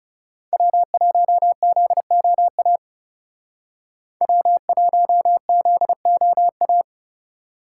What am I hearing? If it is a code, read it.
Morse code, heard as a keyed tone, about 39 wpm, 700 Hz. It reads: W1ZOA W1ZOA